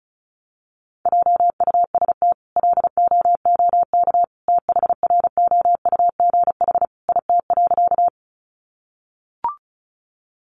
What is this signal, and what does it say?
Morse code, heard as a keyed tone, about 34 wpm, 700 Hz. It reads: JUST LOOK THROUGH IT.
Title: Sentences from Top 200 Words 35wpm